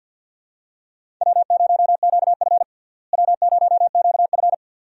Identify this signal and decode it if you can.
Morse code, heard as a keyed tone, about 50 wpm, 700 Hz. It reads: W0QF W0QF
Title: USA Call Signs - 4 Characters - Repeated 2x - 50wpm